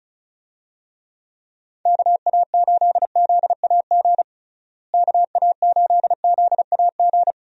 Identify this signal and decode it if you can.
Morse code, heard as a keyed tone, about 36 wpm, 700 Hz. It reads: KA8ZAG KA8ZAG
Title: USA Call Signs - 6 Characters - Repeated 2x - 35wpm